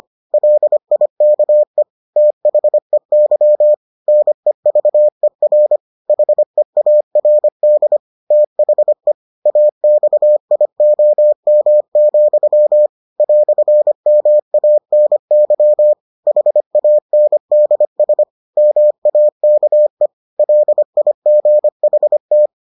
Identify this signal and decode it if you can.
Morse code, heard as a keyed tone, about 25 wpm, 600 Hz. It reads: LIKE THEY NEVER HEARD THE AXIOM, "MANY HANDS MAKE LIGHT